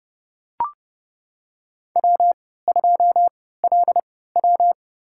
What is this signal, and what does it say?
Morse code, heard as a keyed tone, about 29 wpm, spaced as for 17 wpm, 700 Hz. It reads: W2LW
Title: USA Call Signs - 4 Characters - 3x Character Spacing - 30wpm